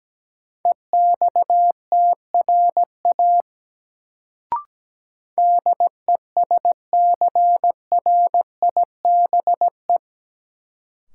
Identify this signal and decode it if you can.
Morse code, heard as a keyed tone, about 17 wpm, 700 Hz. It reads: EXTRA DESCRIBE
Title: Sentences from Top 1000 Words - New Word Drill - 17wpm